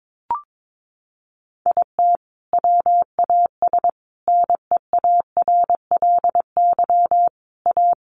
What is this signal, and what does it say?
Morse code, heard as a keyed tone, about 22 wpm, 700 Hz. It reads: IT WAS NEARLY A
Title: Sentences from Top 1300 Words 22wpm